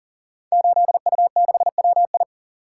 Morse code, heard as a keyed tone, about 40 wpm, 690 Hz. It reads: 8U6WI
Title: Code Groups - 5 Letter:Numbers - 40wpm